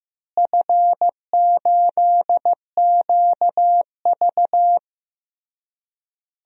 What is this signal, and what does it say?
Morse code, heard as a keyed tone, about 15 wpm, 700 Hz. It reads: F8QV
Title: DXCC-100 Call Signs - 4 Characters - Repeated 3x - 15wpm